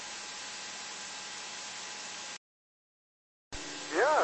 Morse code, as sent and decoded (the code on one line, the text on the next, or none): none